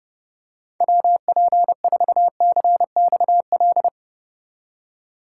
Morse code, heard as a keyed tone, about 28 wpm, 700 Hz. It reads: WP4CXL
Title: USA Call Signs - 6 Characters - 2x Character Spacing - 30wpm